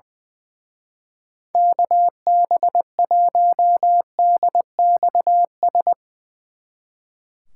Morse code, heard as a keyed tone, about 23 wpm, 700 Hz. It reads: KB1DXS
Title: Speed Racing - USA 2x3 Call Signs 20wpm